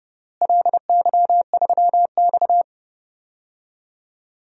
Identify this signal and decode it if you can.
Morse code, heard as a keyed tone, about 30 wpm, 700 Hz. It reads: LY3X
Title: DXCC-100 Call Signs - 4 Characters - Repeated 3x - 30wpm